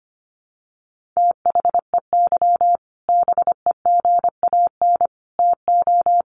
Morse code, heard as a keyed tone, about 25 wpm, 700 Hz. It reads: THEY BEGAN TO
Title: Sentences from Top 1500 Words 25wpm